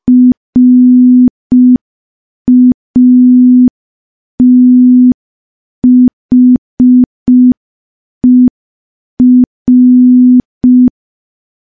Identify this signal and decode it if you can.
Morse code, heard as a keyed tone, about 5 wpm, 250 Hz. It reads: RATHER
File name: morse / morse-training-05WPM